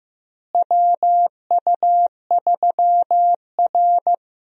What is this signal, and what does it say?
Morse code, heard as a keyed tone, about 16 wpm, 700 Hz. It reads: WU3R
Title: USA Call Signs - 4 Characters - Repeated 2x - 15wpm